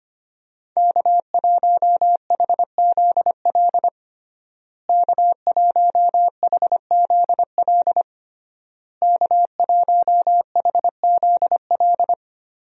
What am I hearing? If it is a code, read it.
Morse code, heard as a keyed tone, about 25 wpm, 700 Hz. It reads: K1HZL K1HZL K1HZL
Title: USA Call Signs - 5 Characters - Repeated 3x - 25wpm